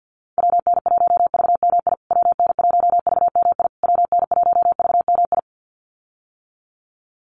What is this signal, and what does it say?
Morse code, heard as a keyed tone, about 51 wpm, 700 Hz. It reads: WN1VMI WN1VMI WN1VMI
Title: USA Call Signs Repeated 3x 50wpm